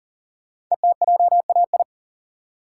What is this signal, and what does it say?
Morse code, heard as a keyed tone, about 39 wpm, 690 Hz. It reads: ETJAI
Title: Code Groups - 5 Letters - 40wpm